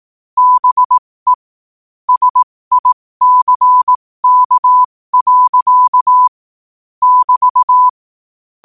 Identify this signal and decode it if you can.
Morse code, heard as a keyed tone, about 18 wpm, spaced as for 15 wpm, 1000 Hz. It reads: BE SICK. <BT>
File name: qotdfiles / 15wpm